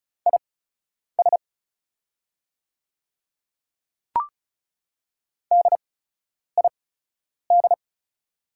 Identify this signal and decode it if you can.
Morse code, heard as a keyed tone, about 36 wpm, spaced as for 10 wpm, 700 Hz. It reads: IS DID
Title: 8x Character Spacing - Top 100 Words - 35wpm